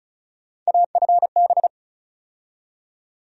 Morse code, heard as a keyed tone, about 34 wpm, 700 Hz. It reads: AFB
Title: USA Call Sign Postfixes - 3 Letters 35wpm